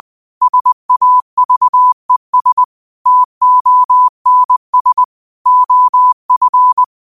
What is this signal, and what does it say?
Morse code, heard as a keyed tone, about 20 wpm, 1000 Hz. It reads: SAVES TONS OF